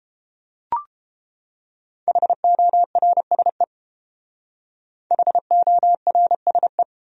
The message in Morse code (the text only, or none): HORSE HORSE